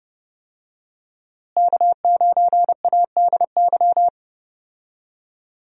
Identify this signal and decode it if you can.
Morse code, heard as a keyed tone, about 30 wpm, 700 Hz. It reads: K9ADY
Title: USA Call Signs - 5 Characters - Repeated 3x - 30wpm